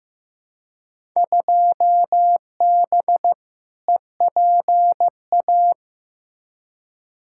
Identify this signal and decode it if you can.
Morse code, heard as a keyed tone, about 15 wpm, 700 Hz. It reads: 2B EPA
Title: Field Day Exchanges Encore - 15wpm